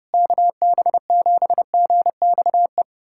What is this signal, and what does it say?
Morse code, heard as a keyed tone, about 30 wpm, 700 Hz. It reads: KB7GXE
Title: USA Callsign DE Callsign - 6 Characters - 30wpm